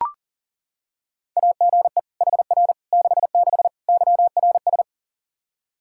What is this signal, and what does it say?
Morse code, heard as a keyed tone, about 40 wpm, 700 Hz. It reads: AGE HR 66 YRS
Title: Practice QSO - Exchange 3 - More Info - 40wpm